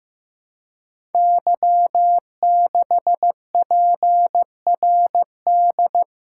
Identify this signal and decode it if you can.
Morse code, heard as a keyed tone, about 15 wpm, 700 Hz. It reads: Y6PRD
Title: Code Groups - 5 Letter:Numbers - 15wpm